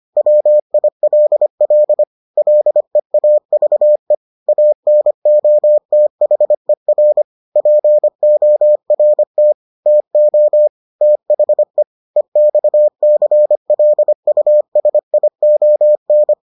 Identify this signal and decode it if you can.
Morse code, heard as a keyed tone, about 25 wpm, 600 Hz. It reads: WILL LEAVE ANOTHER PORT TO THE EXCLUSION